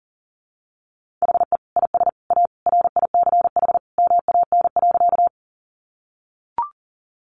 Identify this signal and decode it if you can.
Morse code, heard as a keyed tone, about 40 wpm, 700 Hz. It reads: HE IS A RICH MAN.
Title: Sentences from Top 1100 Words 40wpm